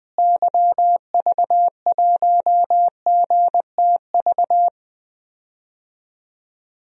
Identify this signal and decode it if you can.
Morse code, heard as a keyed tone, about 20 wpm, 700 Hz. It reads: YV1GTV
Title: DXCC-100 Call Signs - 6 Characters - Repeated 2x - 20wpm